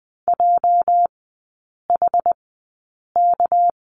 Morse code, heard as a keyed tone, about 20 wpm, spaced as for 8 wpm, 690 Hz. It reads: JHK
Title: 212 Lesson 34e - Number 8 - Three Characters 20wpm